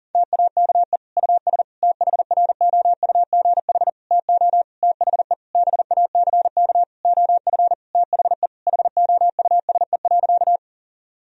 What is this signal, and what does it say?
Morse code, heard as a keyed tone, about 40 wpm, 700 Hz. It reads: TAKE US THROUGH TO THE BACK OF THE HOUSE.
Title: Sentences from Top 1000 Words - Review Part 1 - 40wpm